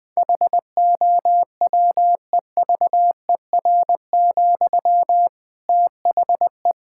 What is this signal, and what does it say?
Morse code, heard as a keyed tone, about 21 wpm, 700 Hz. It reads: HOWEVER, THE
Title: Sentences from Top 1000 Words - Review Part 6 - 20wpm